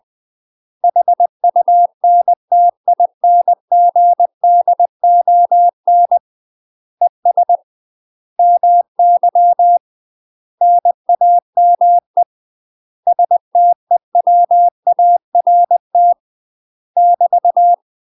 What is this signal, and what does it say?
Morse code, heard as a keyed tone, about 20 wpm, 700 Hz. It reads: HUNTINGDON ES MY NAME STEWART <BT>
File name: UK MASTER QSOs TRAINING / 20WPM / UK MASTER QSOs 0056